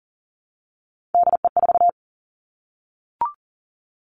Morse code, heard as a keyed tone, about 39 wpm, 700 Hz. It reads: DE4
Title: Three Challenging Letter-Number Combinations 40wpm